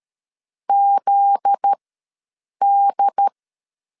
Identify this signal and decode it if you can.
Morse code, heard as a keyed tone, about 13 wpm, spaced as for 7 wpm, 800 Hz. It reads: ZD
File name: LEZIONI / 9 286 ws 12wpm 3351